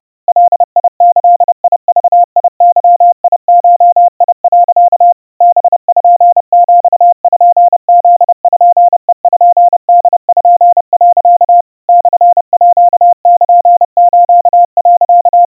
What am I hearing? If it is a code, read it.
Morse code (Japanese Wabun code), heard as a keyed tone, about 30 wpm, 700 Hz. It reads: ガギグゲゴ、 パピプペポ、 モールス、
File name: iroha 100 cpm 700 Hz Crispy